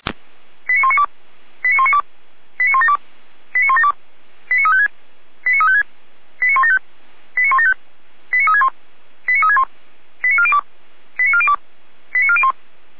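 Five-tone selective call (ZVEI), heard as quick runs of five tones, tones of about 70 ms. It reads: 89192 89192 89172 89172 89356 89356 89175 89175 89361 89361 89402 89402 89401
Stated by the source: ZVEI five-tone call 89370 received by SDR, channel 455.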